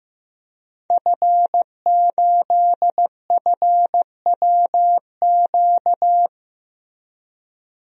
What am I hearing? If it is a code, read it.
Morse code, heard as a keyed tone, about 15 wpm, 700 Hz. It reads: F8FWQ
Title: DXCC-100 Call Signs - 5 Characters - Repeated 3x - 15wpm